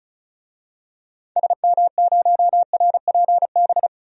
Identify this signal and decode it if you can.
Morse code, heard as a keyed tone, about 36 wpm, 700 Hz. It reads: SM0RPB
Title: DXCC-100 Call Signs - 6 Characters - Repeated 3x - 35wpm